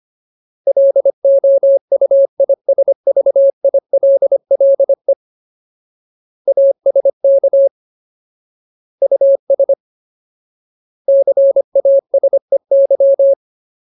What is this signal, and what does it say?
Morse code, heard as a keyed tone, about 25 wpm, 550 Hz. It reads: LOUISVILLE ASK US CASEY
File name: Ultimate Word List / 25WPM Ultimate